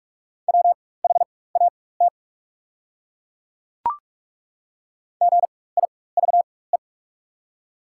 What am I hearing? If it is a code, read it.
Morse code, heard as a keyed tone, about 47 wpm, spaced as for 21 wpm, 700 Hz. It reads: WHAT GIVE
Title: 4x Character Spacing - Top 300 Words - 45wpm